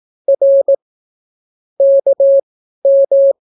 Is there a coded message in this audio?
Yes: Morse code from 0.3 to 3.3 s, heard as a keyed tone, about 18 wpm, spaced as for 12 wpm, 550 Hz. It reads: R KM